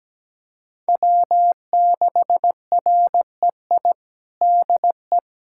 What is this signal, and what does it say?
Morse code, heard as a keyed tone, about 17 wpm, 700 Hz. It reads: W6REI DE
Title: USA Callsign DE Callsign 17wpm